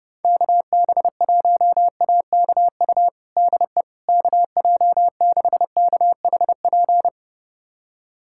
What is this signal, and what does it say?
Morse code, heard as a keyed tone, about 32 wpm, 700 Hz. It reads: KB1AKU DE KJ6KHP
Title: USA Callsign DE Callsign - 6 Characters - 30wpm